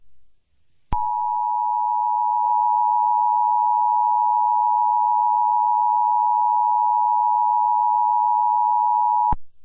Emergency Alert System (SAME) tones (the 853 and 960 Hz attention tone). It is heard from 0.9 to 9.4 s.